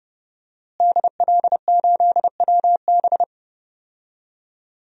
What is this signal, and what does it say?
Morse code, heard as a keyed tone, about 31 wpm, 700 Hz. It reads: DL8WB
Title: DXCC-100 Call Signs - 5 Characters - Repeated 2x - 30wpm